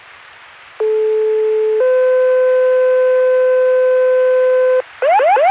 Two-tone page, heard as a one-second tone, then a three-second tone, about 430 then 510 Hz. It is heard from 0.8 to 4.8 s.